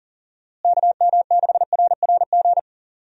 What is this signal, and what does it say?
Morse code, heard as a keyed tone, about 40 wpm, 700 Hz. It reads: KM6RRG
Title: USA Call Signs - 6 Characters - Repeated 2x - 40wpm